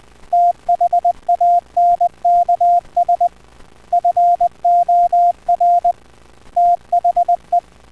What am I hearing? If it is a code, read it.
Morse code, heard as a keyed tone, about 20 wpm, 700 Hz. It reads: THANKS FOR THE